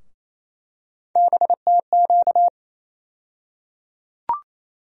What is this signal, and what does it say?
Morse code, heard as a keyed tone, about 27 wpm, 700 Hz. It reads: BTQ
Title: USA Call Sign Postfixes - 3 Letters 28wpm